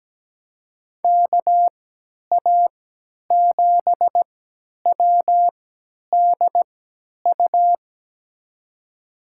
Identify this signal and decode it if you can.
Morse code, heard as a keyed tone, about 17 wpm, spaced as for 10 wpm, 700 Hz. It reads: KA7WDU
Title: USA Call Signs - 6 Characters - 3x Character Spacing - 17wpm